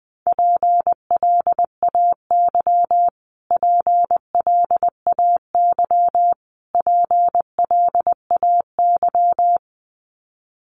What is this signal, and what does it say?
Morse code, heard as a keyed tone, about 19 wpm, 700 Hz. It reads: PLAY PLAY PLAY
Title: Top 200 Words Repeated 20wpm